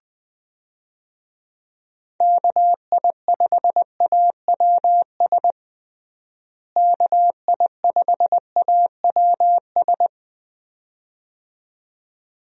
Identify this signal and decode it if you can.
Morse code, heard as a keyed tone, about 21 wpm, 700 Hz. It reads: KI5AWS KI5AWS
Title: USA Call Signs - 6 Characters - Repeated 2x - 20wpm